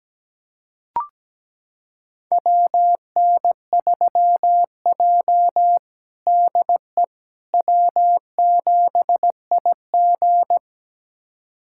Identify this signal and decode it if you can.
Morse code, heard as a keyed tone, about 17 wpm, 700 Hz. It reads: WN3J DE W7IG
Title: USA Callsign DE Callsign - 4 Characters - 17wpm